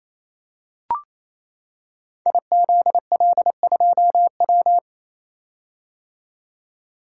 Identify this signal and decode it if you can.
Morse code, heard as a keyed tone, about 27 wpm, 690 Hz. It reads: IZL2W
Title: Code Groups - 5 Letter:Numbers - 28wpm